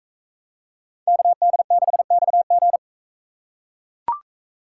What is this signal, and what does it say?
Morse code, heard as a keyed tone, about 40 wpm, 700 Hz. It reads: KD6XG